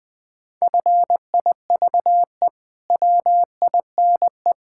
Morse code, heard as a keyed tone, about 20 wpm, 700 Hz. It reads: FIVE WINE
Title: Pangram Challenge v2 - 20wpm